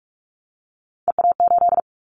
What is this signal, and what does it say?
Morse code, heard as a keyed tone, about 46 wpm, 700 Hz. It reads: EA8
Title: Callsign Prefixes for Top 100 DXCC - 45wpm